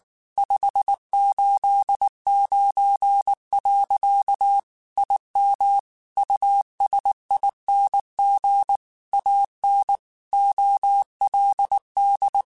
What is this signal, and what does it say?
Morse code, heard as a keyed tone, about 19 wpm, 800 Hz. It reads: 589. IM USING AN OLD
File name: QSOs / QSOs-mp3s / WD0FBT 1@18